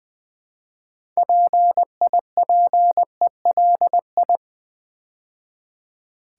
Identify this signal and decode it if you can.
Morse code, heard as a keyed tone, about 20 wpm, 700 Hz. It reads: PIPELI
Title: Build A Word - 2 Letters At A Time - 8-Letter Words - 20wpm